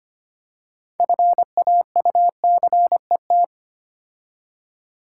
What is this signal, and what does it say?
Morse code, heard as a keyed tone, about 30 wpm, 690 Hz. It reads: FAUCET
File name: Easy Words - Speed-Racing with Hints - 25wpm